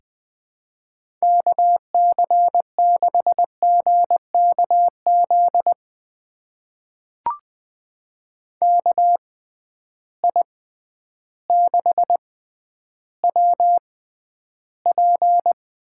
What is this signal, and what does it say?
Morse code, heard as a keyed tone, about 21 wpm, spaced as for 11 wpm, 700 Hz. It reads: KC6GKZ KI6WP
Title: USA Call Signs - 6 Characters - 6x Character Spacing - 20wpm